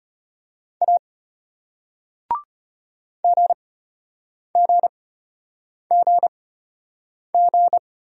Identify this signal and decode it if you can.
Morse code, heard as a keyed tone, about 34 wpm, spaced as for 8 wpm, 700 Hz. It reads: A GGGG